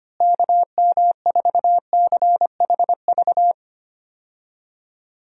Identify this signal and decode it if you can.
Morse code, heard as a keyed tone, about 25 wpm, 700 Hz. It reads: KM4CHV
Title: USA Call Signs - 6 Characters - Repeated 2x - 25wpm